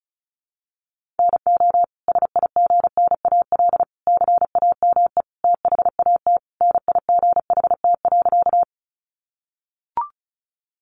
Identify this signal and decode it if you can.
Morse code, heard as a keyed tone, about 34 wpm, 700 Hz. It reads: NO SIGNAL CAME THAT NIGHT.
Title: Sentences from Top 1400 Words 35wpm